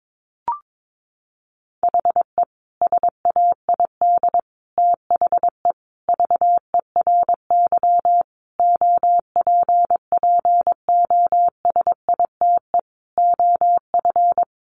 Morse code, heard as a keyed tone, about 22 wpm, 700 Hz. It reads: HE SAID THE VERY OPPOSITE OF
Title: Sentences from Top 1300 Words 22wpm